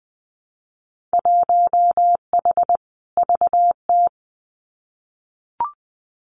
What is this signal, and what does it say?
Morse code, heard as a keyed tone, about 20 wpm, 700 Hz. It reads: 1H VT
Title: Winter Field Day Exchanges Encore 20wpm